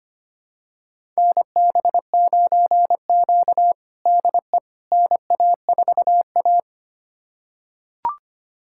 Morse code, heard as a keyed tone, about 25 wpm, 700 Hz. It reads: NB9Q DE NA4A
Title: USA Callsign DE Callsign - 4 Characters - 25wpm